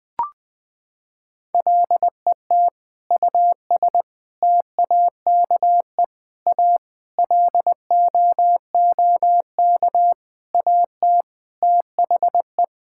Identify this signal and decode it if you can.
Morse code, heard as a keyed tone, about 21 wpm, 700 Hz. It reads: LET US TAKE A LOOK AT THE
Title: Sentences with 5-Word Phrases - 20wpm